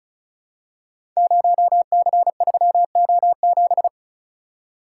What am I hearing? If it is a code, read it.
Morse code, heard as a keyed tone, about 36 wpm, 690 Hz. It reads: 0C3O7
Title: Code Groups - 5 Letter:Numbers - 35wpm